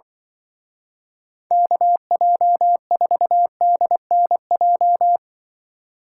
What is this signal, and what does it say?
Morse code, heard as a keyed tone, about 23 wpm, 700 Hz. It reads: KJ4DNJ
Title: Speed Racing - USA 2x3 Call Signs 20wpm